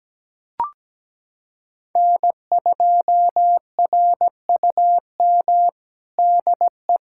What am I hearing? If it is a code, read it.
Morse code, heard as a keyed tone, about 17 wpm, 700 Hz. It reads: N2RUM DE
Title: USA Callsign DE Callsign - 5 Characters - 17wpm